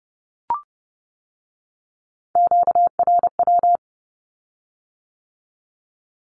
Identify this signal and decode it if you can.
Morse code, heard as a keyed tone, about 30 wpm, 700 Hz. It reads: QRW
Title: Q-Codes with Spoken Meaning - 30wpm